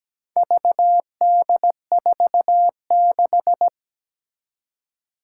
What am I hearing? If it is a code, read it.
Morse code, heard as a keyed tone, about 17 wpm, 700 Hz. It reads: VD46
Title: Four Challenging Letter-Number Combinations 17wpm